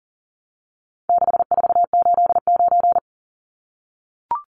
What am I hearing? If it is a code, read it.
Morse code, heard as a keyed tone, about 40 wpm, 700 Hz. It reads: 6489
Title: Numbers - 4-Digit 40wpm